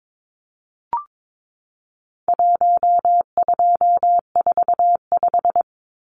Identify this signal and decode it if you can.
Morse code, heard as a keyed tone, about 22 wpm, 700 Hz. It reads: 1245
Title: Numbers - Contest Style - 4-Digit 22wpm